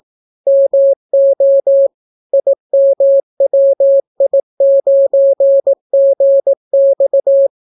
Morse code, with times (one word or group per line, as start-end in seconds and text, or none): 0.5-1.9 MO
2.3-7.5 IMWI9GX